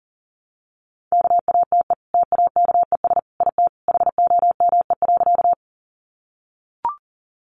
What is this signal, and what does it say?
Morse code, heard as a keyed tone, about 40 wpm, 700 Hz. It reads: KATE TAKES IT HOME.
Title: Easy Phrases 40wpm